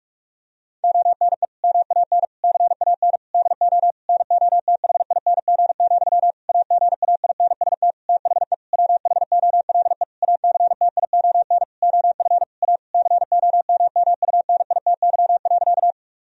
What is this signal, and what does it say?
Morse code, heard as a keyed tone, about 42 wpm, 700 Hz. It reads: ONE MAN CAN DO NOTHING, AGAINST THE WHOLE ACTION OF A COMMUNITY.
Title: Sentences from Top 1000 Words - Review Part 4 - 45wpm